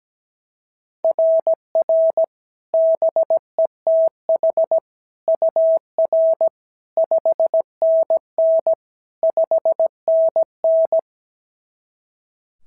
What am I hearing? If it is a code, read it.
Morse code, heard as a keyed tone, about 16 wpm, 650 Hz. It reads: RR BETH UR 5NN 5NN
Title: SOTA QSO Practice 17wpm